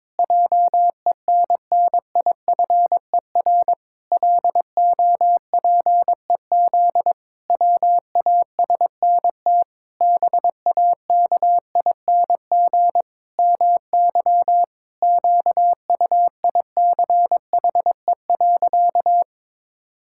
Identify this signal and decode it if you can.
Morse code, heard as a keyed tone, about 22 wpm, 700 Hz. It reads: JENNIFER LOPEZ WASNT BAKING MY QUICHE.
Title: Pangram Challenge v2 - 22wpm